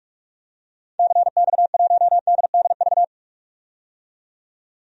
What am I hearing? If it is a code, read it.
Morse code, heard as a keyed tone, about 47 wpm, 700 Hz. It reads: CX1DDV
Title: DXCC-100 Call Signs - 6 Characters - 45wpm